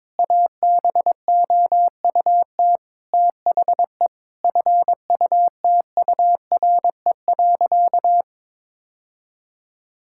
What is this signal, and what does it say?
Morse code, heard as a keyed tone, about 21 wpm, 700 Hz. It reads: ABOUT THE FUTURE.
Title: Sentences from Top 1000 Words - Review Part 5 - 22wpm